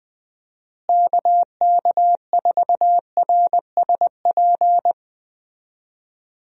Morse code, heard as a keyed tone, about 20 wpm, 700 Hz. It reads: KK4RSP
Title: USA Call Signs - 6 Characters - Repeated 2x - 20wpm